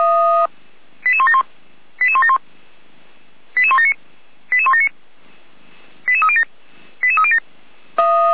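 Five-tone selective call (ZVEI), heard as quick runs of five tones, tones of about 70 ms. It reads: 88171 88171 88179 88179 88307 88307
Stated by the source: ZVEI five-tone selective call 88191 received via SDR.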